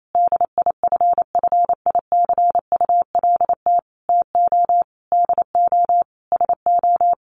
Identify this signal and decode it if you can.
Morse code, heard as a keyed tone, about 28 wpm, 700 Hz. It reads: DIFFICULT TO DO SO
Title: Sentences from Top 600 Words 28wpm